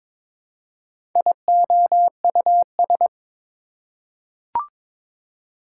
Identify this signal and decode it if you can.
Morse code, heard as a keyed tone, about 22 wpm, 700 Hz. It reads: IOUS
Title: Prefixes and Suffixes 22wpm